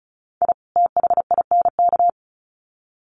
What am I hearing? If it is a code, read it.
Morse code, heard as a keyed tone, about 34 wpm, 700 Hz. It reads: I THINK
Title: Sets of 2 Words 35wpm